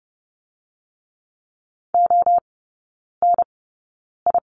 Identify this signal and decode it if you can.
Morse code, heard as a keyed tone, about 30 wpm, spaced as for 9 wpm, 690 Hz. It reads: ONI
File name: Lesson 17e - Letter u - Three Characters 30wpm